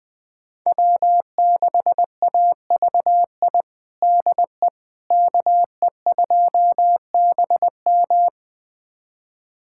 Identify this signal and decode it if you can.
Morse code, heard as a keyed tone, about 20 wpm, 700 Hz. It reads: W6AVI DE KE2BM
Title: USA Callsign DE Callsign - 5 Characters - 20wpm